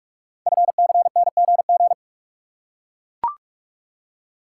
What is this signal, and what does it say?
Morse code, heard as a keyed tone, about 44 wpm, 700 Hz. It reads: FCNGG
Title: Code Groups - 5 Letters - 45wpm